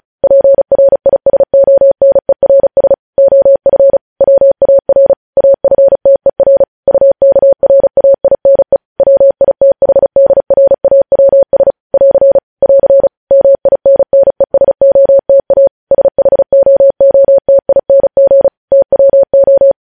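Morse code, heard as a keyed tone, about 36 wpm, 550 Hz. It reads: PRISONERS OF WAR AFTER UKRAINE WITHDRAWS <AR> <AR> MINNESOTA SHOOTING TWO
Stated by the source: Morse news headlines sent at 35 words per minute.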